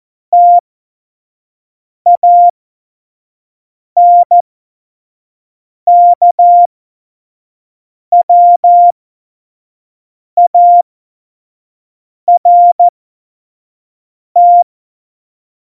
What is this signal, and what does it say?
Morse code, heard as a keyed tone, about 14 wpm, spaced as for 5 wpm, 700 Hz. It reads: TANKWART
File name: Morsekurs / Witze / Witz 28 Juli 2020, 25 F70